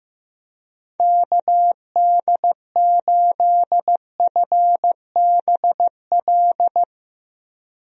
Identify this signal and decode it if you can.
Morse code, heard as a keyed tone, about 15 wpm, 700 Hz. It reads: KD8FBL
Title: USA Call Signs - 6 Characters - 15wpm